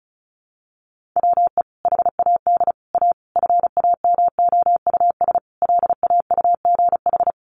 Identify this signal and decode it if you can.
Morse code, heard as a keyed tone, about 34 wpm, 700 Hz. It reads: WE HAD A FAMOUS LAUGH
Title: Sentences from Top 1500 Words 35wpm